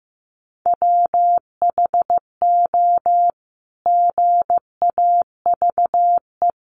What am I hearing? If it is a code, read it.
Morse code, heard as a keyed tone, about 15 wpm, 700 Hz. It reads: WHO GAVE
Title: Sentences from Top 900 Words 15wpm